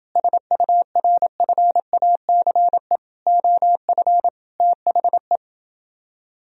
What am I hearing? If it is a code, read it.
Morse code, heard as a keyed tone, about 29 wpm, 700 Hz. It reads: SURFACE OF THE